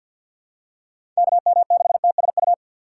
Morse code, heard as a keyed tone, about 53 wpm, 700 Hz. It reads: KM6TSU
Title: USA Call Signs - 6 Characters - 4x Character Spacing - 50wpm